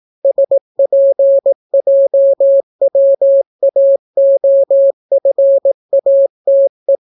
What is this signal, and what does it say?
Morse code, heard as a keyed tone, about 18 wpm, 550 Hz. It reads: SPJWAOFATE